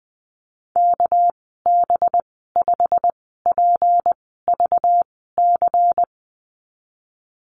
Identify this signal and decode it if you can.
Morse code, heard as a keyed tone, about 19 wpm, spaced as for 14 wpm, 700 Hz. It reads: KB5PVC
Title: 200 Lesson 32g - Letter k - Call Signs 20wpm